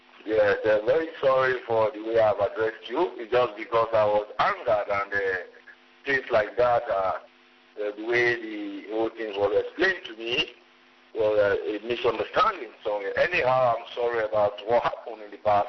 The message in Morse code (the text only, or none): none